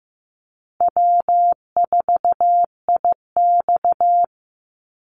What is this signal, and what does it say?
Morse code, heard as a keyed tone, about 15 wpm, 700 Hz. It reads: W4IX
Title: Easy Call Signs Repeated 3x - 15wpm